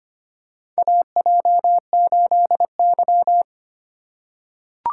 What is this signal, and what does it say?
Morse code, heard as a keyed tone, about 25 wpm, 700 Hz. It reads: AJ8Y
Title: USA Call Signs - 4 Characters - 2x Character Spacing - 25wpm